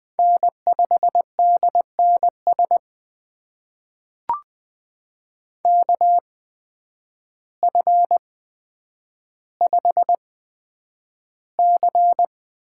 Morse code, heard as a keyed tone, about 20 wpm, spaced as for 9 wpm, 700 Hz. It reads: N5DNS KF5C
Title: USA Call Signs - 8x Character Spacing - 20wpm